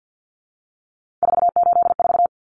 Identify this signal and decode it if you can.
Morse code, heard as a keyed tone, about 50 wpm, 700 Hz. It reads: 484
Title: Numbers - 3-Digit 50wpm